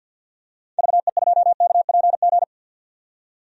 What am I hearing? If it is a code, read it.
Morse code, heard as a keyed tone, about 51 wpm, 700 Hz. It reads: VE2KPG
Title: DXCC-100 Call Signs - 6 Characters - 50wpm